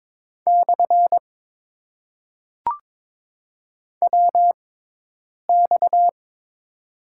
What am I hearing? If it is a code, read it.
Morse code, heard as a keyed tone, about 22 wpm, spaced as for 8 wpm, 700 Hz. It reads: / WX
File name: Top 90 QSO Elements - 6x Character Spacing - 22wpm